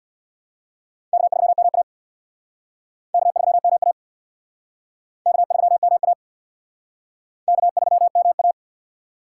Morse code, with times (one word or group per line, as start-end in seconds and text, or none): 1.1-1.8 K3MA
3.1-3.9 K3MA
5.3-6.1 K3MA
7.5-8.5 K3MA